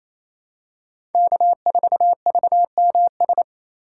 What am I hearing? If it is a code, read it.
Morse code, heard as a keyed tone, about 28 wpm, 700 Hz. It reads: K4VMS
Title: USA Call Signs - 5 Characters - 2x Character Spacing - 28wpm